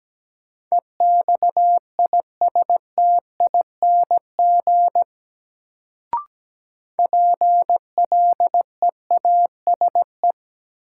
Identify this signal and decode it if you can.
Morse code, heard as a keyed tone, about 17 wpm, 700 Hz. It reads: EXISTING PLEASE
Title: Sentences from Top 1500 Words - New Word Drill - 17wpm